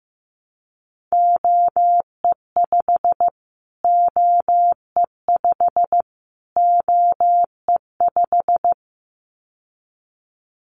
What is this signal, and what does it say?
Morse code, heard as a keyed tone, about 15 wpm, 700 Hz. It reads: OE5 OE5 OE5
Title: Callsign Prefixes for Top 25 DXCC - Repeated 3x - 15wpm